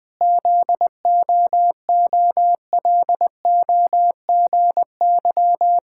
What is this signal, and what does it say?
Morse code, heard as a keyed tone, about 20 wpm, 700 Hz. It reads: ZOOLOGY
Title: Pangram Challenge v2 - 20wpm